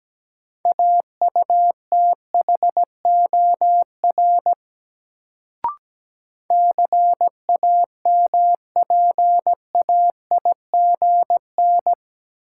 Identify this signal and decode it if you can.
Morse code, heard as a keyed tone, about 17 wpm, 700 Hz. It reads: AUTHOR CAMPAIGN
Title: Sentences from Top 1200 Words - New Word Drill Alt - 17wpm